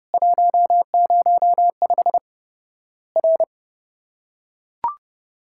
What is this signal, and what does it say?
Morse code, heard as a keyed tone, about 32 wpm, 700 Hz. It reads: 105 R
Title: Worked All Europe DX Contest - QTC Exchange - 30wpm